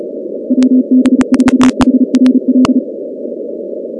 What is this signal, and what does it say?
Morse code, heard as a keyed tone, about 35 wpm, 260 Hz. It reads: WB4HRL
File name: contests / 2019-cw-wpx-kt5j-k5pi- -2330 0-16